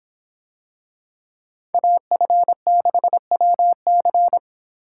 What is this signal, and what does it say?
Morse code, heard as a keyed tone, about 24 wpm, 700 Hz. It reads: AF6WC